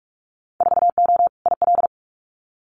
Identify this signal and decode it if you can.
Morse code, heard as a keyed tone, about 46 wpm, 700 Hz. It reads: 4O IL
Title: Winter Field Day Exchanges Encore 45wpm